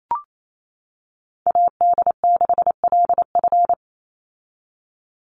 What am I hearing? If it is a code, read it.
Morse code, heard as a keyed tone, about 28 wpm, 700 Hz. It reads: AD6LF
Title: USA Call Signs 28wpm